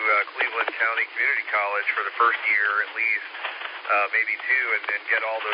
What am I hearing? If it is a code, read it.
Emergency Alert System (SAME) tones, heard as harsh buzzing bursts, with speech. It is heard from 0.7 to 4.8 s.